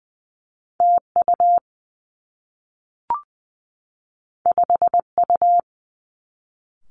Morse code, heard as a keyed tone, about 20 wpm, 700 Hz. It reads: T U 5 U
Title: Two Challenging Letter-Number Combinations 20wpm